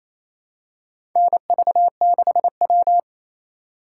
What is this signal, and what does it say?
Morse code, heard as a keyed tone, about 28 wpm, 700 Hz. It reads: NV6W
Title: USA Call Signs - 4 Characters - 4x Character Spacing - 28wpm